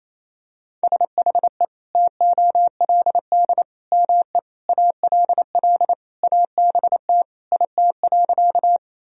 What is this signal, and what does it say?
Morse code, heard as a keyed tone, about 28 wpm, 700 Hz. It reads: SHE TOLD ME ALL ABT IT.
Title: Sentences from Top 400 Words and Abbreviations 28wpm